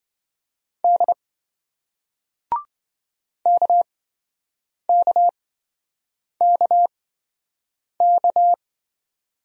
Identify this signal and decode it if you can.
Morse code, heard as a keyed tone, about 26 wpm, spaced as for 7 wpm, 700 Hz. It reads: D KKKK